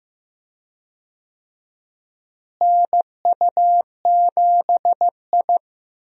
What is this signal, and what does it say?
Morse code, heard as a keyed tone, about 15 wpm, 700 Hz. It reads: NU7I
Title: USA Call Signs - 4 Characters - Repeated 2x - 15wpm